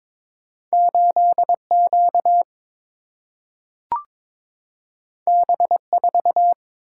Morse code, heard as a keyed tone, about 22 wpm, 700 Hz. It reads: 8 Q B 4
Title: Single-Letter Single-Number 22wpm